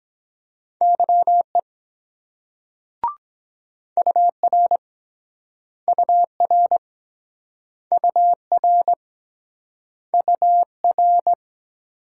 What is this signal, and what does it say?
Morse code, heard as a keyed tone, about 23 wpm, 700 Hz. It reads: YE UR UR UR UR